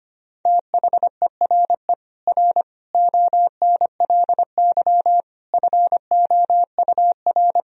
Morse code, heard as a keyed tone, about 25 wpm, 700 Hz. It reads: THERE R ONLY FOUR
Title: Sentences from Top 300 Words and Abbreviations 25wpm